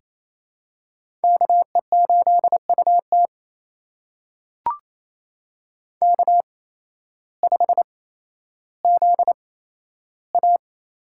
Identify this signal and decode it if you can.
Morse code, heard as a keyed tone, about 28 wpm, spaced as for 13 wpm, 700 Hz. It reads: KE8UT K5ZA
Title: USA Call Signs - 5 Characters - 8x Character Spacing - 28wpm